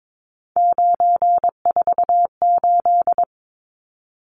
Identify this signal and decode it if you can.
Morse code, heard as a keyed tone, about 22 wpm, 700 Hz. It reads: 948
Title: Numbers - 3-Digit 22wpm